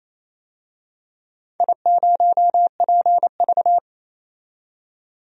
Morse code, heard as a keyed tone, about 29 wpm, 700 Hz. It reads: I0PV
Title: DXCC-100 Call Signs - 4 Characters - 28wpm